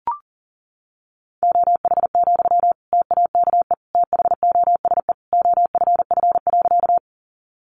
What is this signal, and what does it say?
Morse code, heard as a keyed tone, about 40 wpm, 700 Hz. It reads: OH, TAKE THOSE OFF.
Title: Sentences from Top 500 Words 40wpm